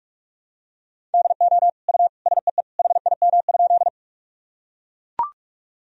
Morse code, heard as a keyed tone, about 41 wpm, 700 Hz. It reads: DO U SEE HIM?
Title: Sentences from Top 100 Words and Abbreviations 45wpm